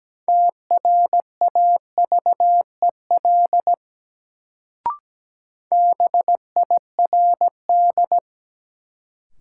Morse code, heard as a keyed tone, about 17 wpm, 700 Hz. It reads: TRAVEL BIRD
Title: Sentences from Top 400 Words - New Word Drill - 17wpm